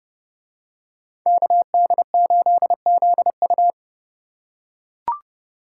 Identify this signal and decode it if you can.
Morse code, heard as a keyed tone, about 29 wpm, 700 Hz. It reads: KD8ZU